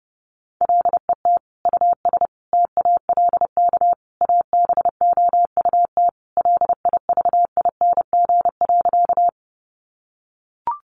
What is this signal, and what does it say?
Morse code, heard as a keyed tone, about 30 wpm, 700 Hz. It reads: LET US TALK ABOUT LIVING.
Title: Sentences from Top 1100 Words 30wpm